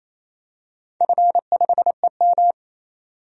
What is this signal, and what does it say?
Morse code, heard as a keyed tone, about 28 wpm, 700 Hz. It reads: F5EM
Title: DXCC-100 Call Signs - 4 Characters - 28wpm